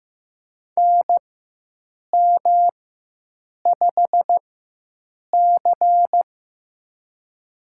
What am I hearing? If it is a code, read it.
Morse code, heard as a keyed tone, about 15 wpm, spaced as for 7 wpm, 700 Hz. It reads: NM5C
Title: USA Call Signs - 4 Characters - 4x Character Spacing - 15wpm